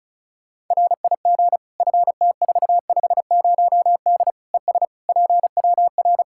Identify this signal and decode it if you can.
Morse code, heard as a keyed tone, about 36 wpm, 700 Hz. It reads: RIG FT450D ES PWR
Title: Practice QSO - Exchange 2 - Info - 35wpm